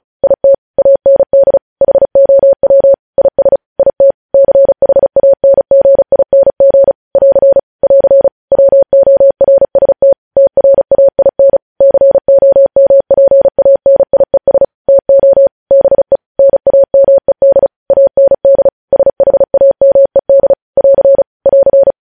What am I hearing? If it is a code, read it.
Morse code, heard as a keyed tone, about 36 wpm, 550 Hz. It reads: IT AND HOW IS IT CHANGING <AR> <AR> WORST TRAIN COMPANIES TO BE NAMED AND SHAMED <AR> <AR>
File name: podcast / News headlines-1x-35x35